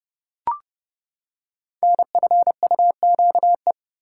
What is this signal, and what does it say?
Morse code, heard as a keyed tone, about 32 wpm, 700 Hz. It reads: NFUQE
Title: Code Groups - 5 Letter:Numbers - 30wpm